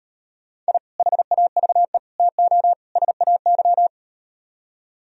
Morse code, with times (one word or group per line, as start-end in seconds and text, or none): 0.7-0.8 I
1.0-2.0 HAVE
2.2-2.7 TO
3.0-3.9 SAY